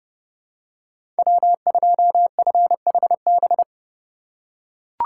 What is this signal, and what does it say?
Morse code, heard as a keyed tone, about 30 wpm, 700 Hz. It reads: W2FHB
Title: USA Call Signs - 2x Character Spacing - 30wpm